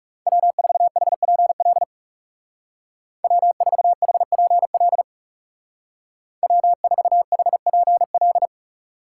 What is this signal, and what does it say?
Morse code, heard as a keyed tone, about 35 wpm, 700 Hz. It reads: W4HPL W4HPL W4HPL
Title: Speed Racing - USA 1x3 Call Signs 30wpm